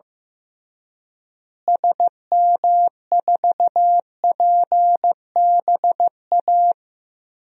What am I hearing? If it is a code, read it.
Morse code, heard as a keyed tone, about 15 wpm, 700 Hz. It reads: SM4PBA
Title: DXCC-100 Call Signs - 6 Characters - Repeated 3x - 15wpm